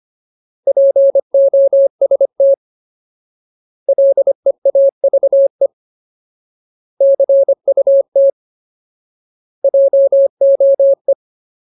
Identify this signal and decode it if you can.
Morse code, heard as a keyed tone, about 25 wpm, 550 Hz. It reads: POST LEAVE CUT JOE
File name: Ultimate Word List / 25WPM Ultimate